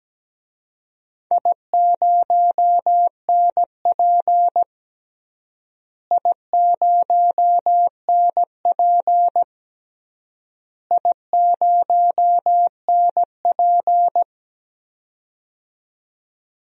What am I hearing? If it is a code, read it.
Morse code, heard as a keyed tone, about 17 wpm, 700 Hz. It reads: I0NP I0NP I0NP
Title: DXCC-100 Call Signs - 4 Characters - Repeated 3x - 17wpm